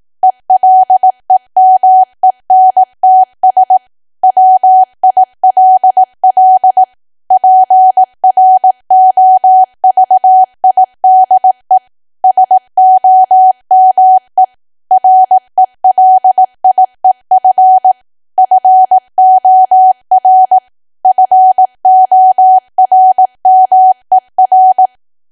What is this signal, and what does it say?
Morse code, heard as a keyed tone, about 18 wpm, 750 Hz. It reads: ELEMENTS WILL PROVIDE SOME RELIEF FOR FORMER
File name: cwa / W1AW-Level2 / 18WPM